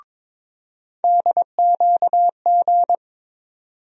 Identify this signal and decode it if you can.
Morse code, heard as a keyed tone, about 22 wpm, 700 Hz. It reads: DQG
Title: USA Call Sign Postfixes - 3 Letters 22wpm